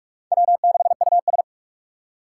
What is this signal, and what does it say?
Morse code, heard as a keyed tone, about 40 wpm, 700 Hz. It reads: W6US